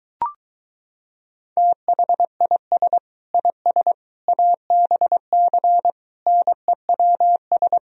Morse code, heard as a keyed tone, about 23 wpm, 700 Hz. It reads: THIS IS ABC NEWS